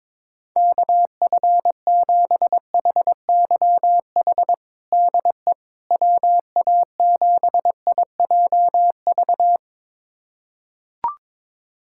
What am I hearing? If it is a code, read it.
Morse code, heard as a keyed tone, about 22 wpm, 700 Hz. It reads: KF7HYH DE WA7IJV
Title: USA Callsign DE Callsign - 6 Characters - 22wpm